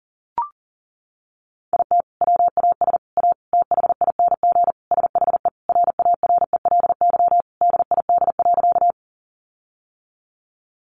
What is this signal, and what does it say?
Morse code, heard as a keyed tone, about 40 wpm, 700 Hz. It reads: IT WAS A THING SHE RARELY DID.
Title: Sentences from Top 1400 Words 40wpm